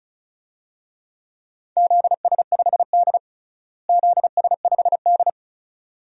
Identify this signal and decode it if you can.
Morse code, heard as a keyed tone, about 36 wpm, 700 Hz. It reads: ZS5D ZS5D
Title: DXCC-100 Call Signs - 4 Characters - Repeated 2x - 35wpm